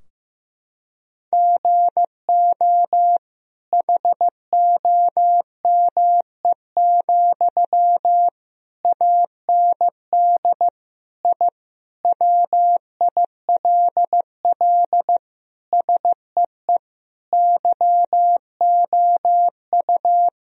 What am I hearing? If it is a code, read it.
Morse code, heard as a keyed tone, about 15 wpm, 700 Hz. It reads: GO HOME, AND I WILL SEE YOU
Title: Sentences from Top 1000 Words - Review Part 4 - 15wpm